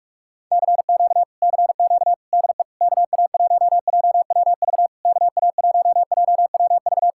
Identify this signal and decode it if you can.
Morse code, heard as a keyed tone, about 40 wpm, 700 Hz. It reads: CQ CQ DE KA1JWV KA1JWV